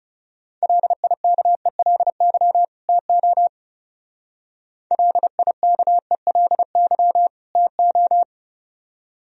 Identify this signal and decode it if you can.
Morse code, heard as a keyed tone, about 36 wpm, 700 Hz. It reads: LIKELY TO LIKELY TO
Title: Speed-Racing - Sets of 2 Words 30wpm